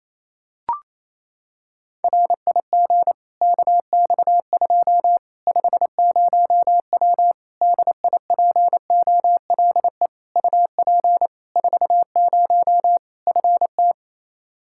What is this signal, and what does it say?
Morse code, heard as a keyed tone, about 27 wpm, 700 Hz. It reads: RIG KX2 50W DIPOLE UP 40 FT
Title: Practice QSO - Exchange 2 - Info - 28wpm